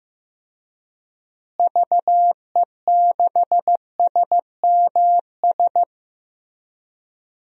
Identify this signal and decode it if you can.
Morse code, heard as a keyed tone, about 15 wpm, 700 Hz. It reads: VE6SMS
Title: DXCC-100 Call Signs - 6 Characters - Repeated 2x - 15wpm